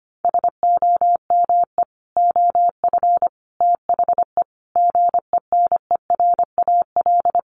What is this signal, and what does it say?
Morse code, heard as a keyed tone, about 25 wpm, 700 Hz. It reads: SOME OF THE GENERAL
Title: Sentences from Top 800 Words 25wpm